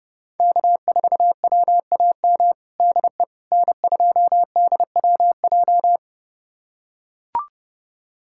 Morse code, heard as a keyed tone, about 30 wpm, 700 Hz. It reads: K4WAM DE N2DWJ
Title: USA Callsign DE Callsign - 5 Characters - 30wpm